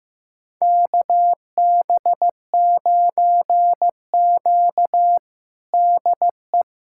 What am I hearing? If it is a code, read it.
Morse code, heard as a keyed tone, about 15 wpm, 700 Hz. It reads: KB9Q DE
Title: USA Callsign DE Callsign - 4 Characters - 15wpm